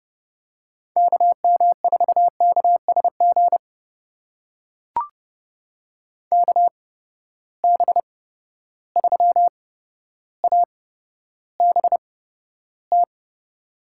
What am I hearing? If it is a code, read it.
Morse code, heard as a keyed tone, about 30 wpm, spaced as for 13 wpm, 700 Hz. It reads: KM4KSG KB3ABT
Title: USA Call Signs - 6 Characters - 8x Character Spacing - 30wpm